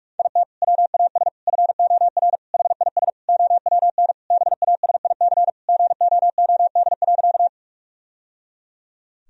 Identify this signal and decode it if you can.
Morse code, heard as a keyed tone, about 43 wpm, 700 Hz. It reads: IT WAS FOR HIS OWN BASIC GOOD.
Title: Sentences from Top 1000 Words - Review Part 3 - 45wpm